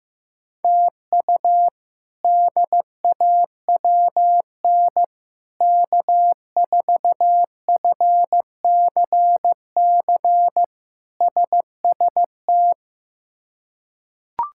Morse code, heard as a keyed tone, about 15 wpm, 700 Hz. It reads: TU DAWN K4FCC SST
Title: K1USN SST QSO Practice - 15wpm